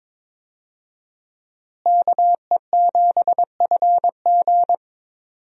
Morse code, heard as a keyed tone, about 22 wpm, 700 Hz. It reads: KE7FG
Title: USA Call Signs - 5 Characters - Repeated 2x - 22wpm